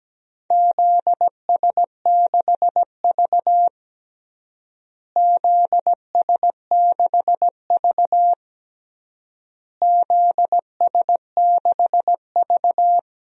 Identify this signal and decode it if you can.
Morse code, heard as a keyed tone, about 17 wpm, 700 Hz. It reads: ZS6V ZS6V ZS6V
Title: DXCC-100 Call Signs - 4 Characters - Repeated 3x - 17wpm